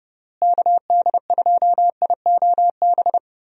Morse code, heard as a keyed tone, about 32 wpm, 700 Hz. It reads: KD2IOB
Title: USA Call Signs - 6 Characters - Repeated 3x - 30wpm